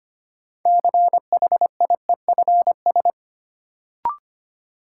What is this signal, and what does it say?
Morse code, heard as a keyed tone, about 25 wpm, 700 Hz. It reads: CHIEFS
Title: Sets of 5 Words Encore - New Word Drill - 25wpm